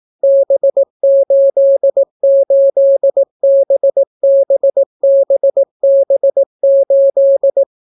Morse code, heard as a keyed tone, about 18 wpm, 550 Hz. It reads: B88BBBB8